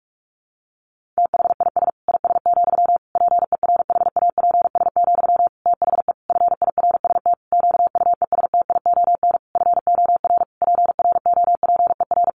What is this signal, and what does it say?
Morse code, heard as a keyed tone, about 44 wpm, 700 Hz. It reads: THIS IS, PERHAPS, THE FIRST QUESTION FOR PROPER
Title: Sentences from Top 1500 Words 45wpm